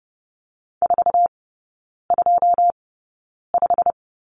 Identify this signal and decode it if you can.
Morse code, heard as a keyed tone, about 30 wpm, spaced as for 9 wpm, 700 Hz. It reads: 425
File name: Lesson 18e - Letter m - Three Characters 30wpm